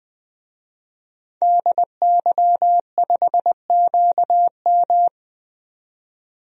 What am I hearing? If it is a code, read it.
Morse code, heard as a keyed tone, about 21 wpm, 700 Hz. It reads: DY5QM
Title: DXCC-100 Call Signs - 5 Characters - 20wpm